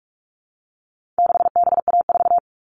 Morse code, heard as a keyed tone, about 44 wpm, 700 Hz. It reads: 6BA4
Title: All Cut Numbers - 4-Digit 45wpm